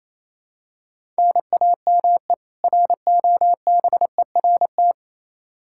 Morse code, heard as a keyed tone, about 28 wpm, 700 Hz. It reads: NAME ROBERT
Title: Practice QSO - Exchange 1 - Beginning - 28wpm